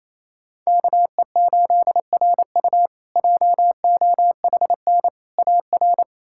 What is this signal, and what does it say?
Morse code, heard as a keyed tone, about 27 wpm, 700 Hz. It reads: KE8RU JOHN AR
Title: K1USN SST QSO Practice - 28wpm